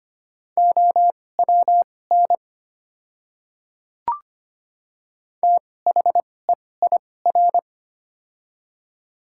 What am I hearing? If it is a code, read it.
Morse code, heard as a keyed tone, about 25 wpm, spaced as for 18 wpm, 700 Hz. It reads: OWN THEIR
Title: 2x Character Spacing - Top 500 Words - 25wpm